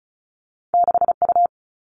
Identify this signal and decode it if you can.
Morse code, heard as a keyed tone, about 36 wpm, 700 Hz. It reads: 6U
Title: All Cut Numbers - 2-Digit 35wpm